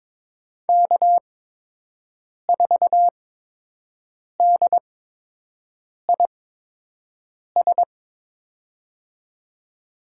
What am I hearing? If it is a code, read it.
Morse code, heard as a keyed tone, about 22 wpm, spaced as for 6 wpm, 700 Hz. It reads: K4DIS
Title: USA Call Signs - 5 Characters - 8x Character Spacing - 22wpm